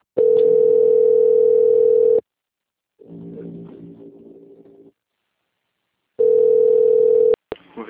Ringback tone (North America), 440 + 480 Hz.
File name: out- - - - - .30678